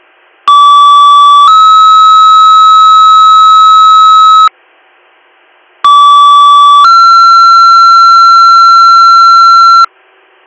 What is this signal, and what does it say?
Two-tone page, heard as a one-second tone, then a three-second tone, about 1150 then 1290 Hz, about 1150 then 1360 Hz. It is heard from 0.5 to 9.9 s.